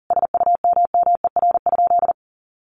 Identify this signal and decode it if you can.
Morse code, heard as a keyed tone, about 40 wpm, 700 Hz. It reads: SUMMER?
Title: Sentences from Top 1200 Words 40wpm